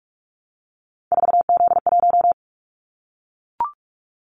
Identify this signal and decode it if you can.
Morse code, heard as a keyed tone, about 44 wpm, 700 Hz. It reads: 4Z1
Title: Callsign Prefixes for Top 50 DXCC - 45wpm